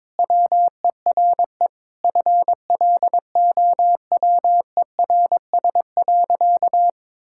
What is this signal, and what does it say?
Morse code, heard as a keyed tone, about 22 wpm, 700 Hz. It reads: WERE FLOWERS.
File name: Sentences with 9-Word Phrases - 22wpm